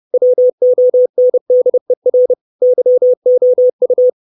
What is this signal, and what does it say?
Morse code, heard as a keyed tone, about 30 wpm, 500 Hz. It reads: WONDER YOU
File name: 632 Words 30WPM